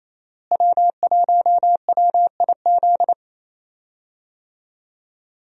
Morse code, heard as a keyed tone, about 28 wpm, 700 Hz. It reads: W1WIZ
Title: USA Call Signs - 5 Characters - Repeated 3x - 28wpm